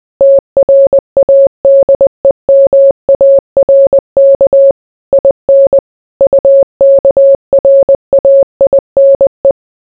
Morse code, heard as a keyed tone, about 20 wpm, 550 Hz. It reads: TRADEMARK IN UKRAINE